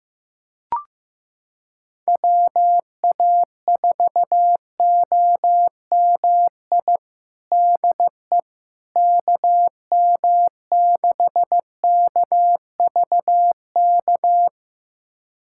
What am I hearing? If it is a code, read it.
Morse code, heard as a keyed tone, about 15 wpm, 700 Hz. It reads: WA4OMI DE KM6KVK
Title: USA Callsign DE Callsign - 6 Characters - 15wpm